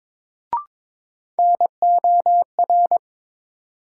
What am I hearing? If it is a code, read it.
Morse code, heard as a keyed tone, about 22 wpm, 700 Hz. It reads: NOR
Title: Sentences from Top 1500 Words - New Word Drill Alt - 22wpm